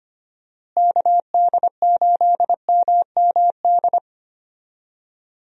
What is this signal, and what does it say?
Morse code, heard as a keyed tone, about 25 wpm, 700 Hz. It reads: KD8MMD
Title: USA Call Signs - 6 Characters - Repeated 3x - 25wpm